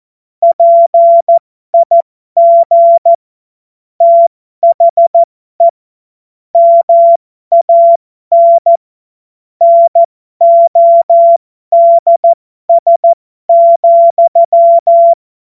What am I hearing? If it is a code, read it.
Morse code, heard as a keyed tone, about 14 wpm, spaced as for 12 wpm, 680 Hz. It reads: PIG THE MAN NODS,
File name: Morsekurs / Witze